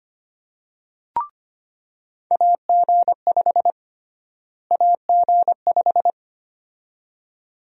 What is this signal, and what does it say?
Morse code, heard as a keyed tone, about 25 wpm, 700 Hz. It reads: AG5 AG5
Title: USA Call Sign Prefixes - Repeated 2x - 25wpm